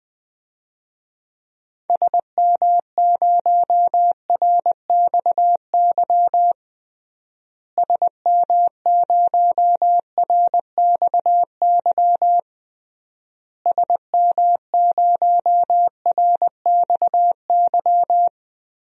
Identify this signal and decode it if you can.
Morse code, heard as a keyed tone, about 20 wpm, 700 Hz. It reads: SM0RXY SM0RXY SM0RXY
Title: DXCC-100 Call Signs - 6 Characters - Repeated 3x - 20wpm